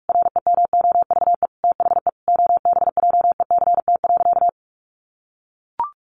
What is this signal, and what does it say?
Morse code, heard as a keyed tone, about 44 wpm, 700 Hz. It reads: REMOVE THE OBJECT.
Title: Sentences from Top 800 Words 45wpm